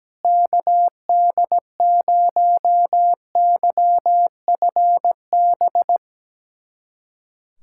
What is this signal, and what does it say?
Morse code, heard as a keyed tone, about 17 wpm, 700 Hz. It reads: KD0YFB
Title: USA Callsign DE Callsign - 6 Characters - 17wpm